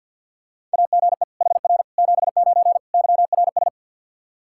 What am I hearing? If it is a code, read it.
Morse code, heard as a keyed tone, about 48 wpm, 700 Hz. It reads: AGE HR 79 YRS
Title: K1USN SST QSO Practice 50wpm